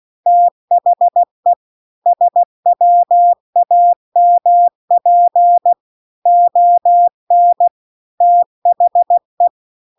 Morse code, heard as a keyed tone, about 16 wpm, 700 Hz. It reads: THE SWAMP ON THE